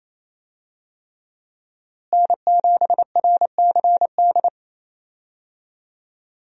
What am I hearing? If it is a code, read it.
Morse code, heard as a keyed tone, about 28 wpm, 700 Hz. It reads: N7RCD
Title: USA Call Signs - 5 Characters - Repeated 3x - 28wpm